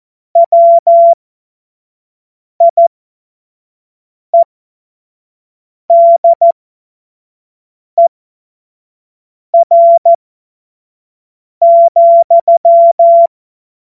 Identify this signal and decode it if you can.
Morse code, heard as a keyed tone, about 14 wpm, spaced as for 5 wpm, 680 Hz. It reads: WIEDER,
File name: Morsekurs / Witze / Witz 16. Dezember 2025 25 F70